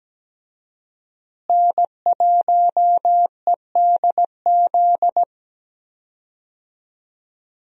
Morse code, heard as a keyed tone, about 17 wpm, 700 Hz. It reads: N1EDZ
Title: USA Call Signs - 5 Characters - Repeated 3x - 17wpm